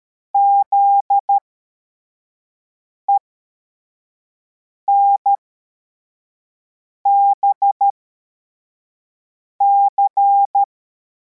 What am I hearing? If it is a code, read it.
Morse code, heard as a keyed tone, about 13 wpm, spaced as for 4 wpm, 800 Hz. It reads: ZENBC
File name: formation / cw / Text55